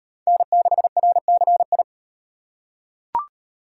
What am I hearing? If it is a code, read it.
Morse code, heard as a keyed tone, about 38 wpm, 700 Hz. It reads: N6RCI